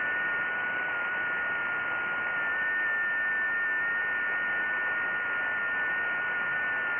FT8.
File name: IC-7100